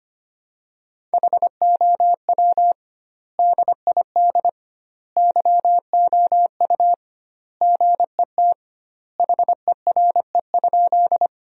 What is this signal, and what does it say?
Morse code, heard as a keyed tone, about 25 wpm, 700 Hz. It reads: HOW DID YOU GET HERE?
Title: 2x Word-Spacing - Sentences from Top 200 Words - 25wpm